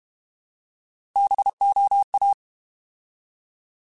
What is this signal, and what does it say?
Morse code, heard as a keyed tone, about 32 wpm, 800 Hz. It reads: BOA
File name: Alphabetical / Alphabetical mp3s / Bs @30